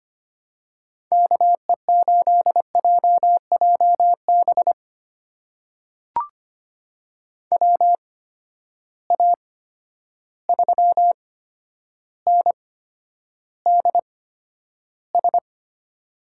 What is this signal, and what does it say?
Morse code, heard as a keyed tone, about 25 wpm, spaced as for 11 wpm, 700 Hz. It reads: KE8JJB WA3NDS
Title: USA Call Signs - 6 Characters - 8x Character Spacing - 25wpm